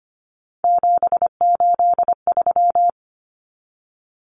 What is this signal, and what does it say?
Morse code, heard as a keyed tone, about 25 wpm, 700 Hz. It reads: 783
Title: Common Cut Numbers - 3-Digit 25wpm